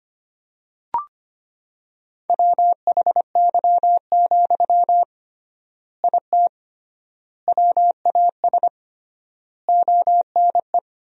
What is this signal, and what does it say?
Morse code, heard as a keyed tone, about 25 wpm, 700 Hz. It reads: WHY, IT WAS ONE
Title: 3x Word-Spacing - Sentences from Top 300 Words - 25wpm